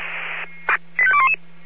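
Five-tone selective call (ZVEI), heard as quick runs of five tones, tones of about 70 ms. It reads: 85211
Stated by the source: ZVEI five-tone selective call received on an SDR.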